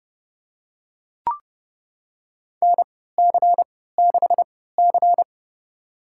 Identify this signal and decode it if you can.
Morse code, heard as a keyed tone, about 29 wpm, spaced as for 17 wpm, 700 Hz. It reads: NC6C
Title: USA Call Signs - 4 Characters - 3x Character Spacing - 30wpm